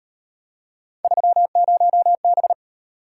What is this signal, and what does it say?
Morse code, heard as a keyed tone, about 37 wpm, 700 Hz. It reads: 30B